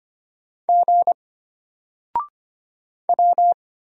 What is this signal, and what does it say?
Morse code, heard as a keyed tone, about 25 wpm, 700 Hz. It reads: G W
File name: Single-Letter-Number-25wpm